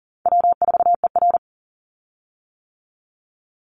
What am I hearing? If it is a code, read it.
Morse code, heard as a keyed tone, about 40 wpm, 700 Hz. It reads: W4ER
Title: Easy Call Signs - 40wpm